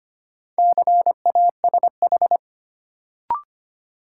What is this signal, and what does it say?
Morse code, heard as a keyed tone, about 25 wpm, 700 Hz. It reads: CASH
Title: Sentences from Top 700 Words - New Word Drill - 25wpm